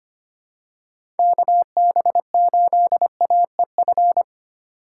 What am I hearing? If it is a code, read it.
Morse code, heard as a keyed tone, about 25 wpm, 700 Hz. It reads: KB8AEF
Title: USA Call Signs - 6 Characters - 4x Character Spacing - 25wpm